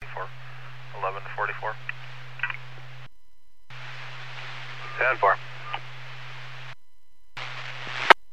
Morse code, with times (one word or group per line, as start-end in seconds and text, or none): none